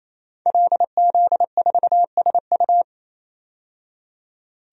Morse code, heard as a keyed tone, about 28 wpm, 700 Hz. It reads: LZ4SU
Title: DXCC-100 Call Signs - 5 Characters - 28wpm